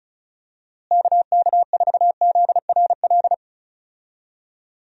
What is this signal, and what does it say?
Morse code, heard as a keyed tone, about 34 wpm, 700 Hz. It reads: KK4ZRL
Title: USA Call Signs - 6 Characters - Repeated 2x - 35wpm